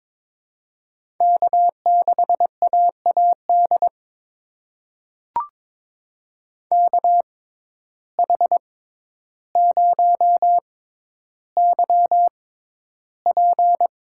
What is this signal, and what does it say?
Morse code, heard as a keyed tone, about 22 wpm, spaced as for 11 wpm, 700 Hz. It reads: K6AAD KH0YP
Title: USA Call Signs - 5 Characters - 6x Character Spacing - 22wpm